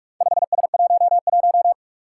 Morse code, heard as a keyed tone, about 44 wpm, 700 Hz. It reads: 5S11
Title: Four Challenging Letter-Number Combinations 45wpm